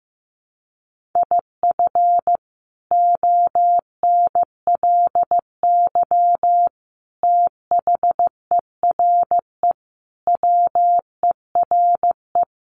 Morse code, heard as a keyed tone, about 15 wpm, 700 Hz. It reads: IF ONLY THERE WERE
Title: Sentences from Top 1400 Words 15wpm